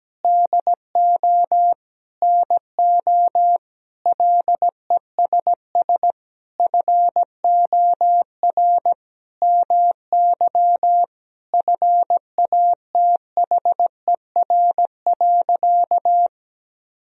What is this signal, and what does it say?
Morse code, heard as a keyed tone, about 17 wpm, 700 Hz. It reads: DO NO LESS FOR MY FATHER.
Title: Sentences from Top 1000 Words - Review Part 2 - 17wpm